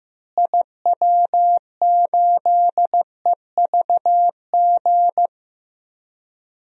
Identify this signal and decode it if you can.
Morse code, heard as a keyed tone, about 15 wpm, 700 Hz. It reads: IW8EVG
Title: DXCC-100 Call Signs - 6 Characters - Repeated 2x - 15wpm